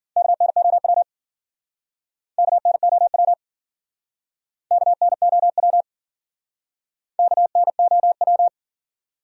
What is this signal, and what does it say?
Morse code, heard as a keyed tone, about 50 wpm, 700 Hz. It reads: KNOW KNOW KNOW KNOW
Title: Speed-Racing Top 200 Words 40wpm